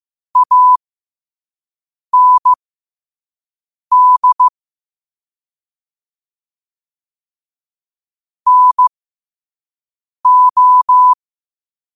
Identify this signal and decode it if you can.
Morse code, heard as a keyed tone, about 15 wpm, spaced as for 5 wpm, 1000 Hz. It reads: AND NO
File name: qotdfiles / 5wpm